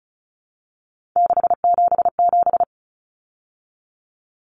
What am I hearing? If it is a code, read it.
Morse code, heard as a keyed tone, about 34 wpm, 700 Hz. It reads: 677
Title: Numbers - 3-Digit 35wpm